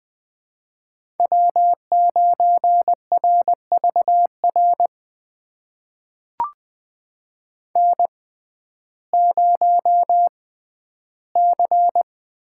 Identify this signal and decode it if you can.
Morse code, heard as a keyed tone, about 20 wpm, spaced as for 12 wpm, 700 Hz. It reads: W9RVR N0C
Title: USA Call Signs - 6x Character Spacing - 20wpm